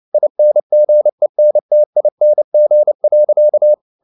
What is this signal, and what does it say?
Morse code, heard as a keyed tone, about 29 wpm, 600 Hz. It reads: INGENTING.
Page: CW med Gnister